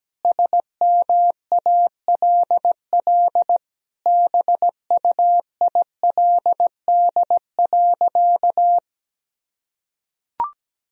Morse code, heard as a keyed tone, about 17 wpm, 700 Hz. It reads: SMALL BUILD.
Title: Sentences from Top 1000 Words - Review Part 1 - 17wpm